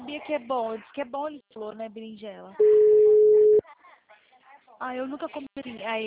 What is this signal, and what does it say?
Ringback tone (Europe, 425 Hz), with speech.